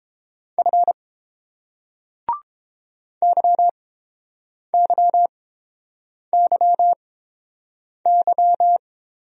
Morse code, heard as a keyed tone, about 30 wpm, spaced as for 7 wpm, 700 Hz. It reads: F YYYY